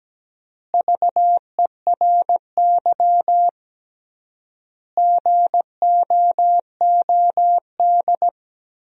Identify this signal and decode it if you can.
Morse code, heard as a keyed tone, about 17 wpm, 700 Hz. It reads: VERY GOOD
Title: 3x Word-Spacing - Sentences from Top 200 Words - 17wpm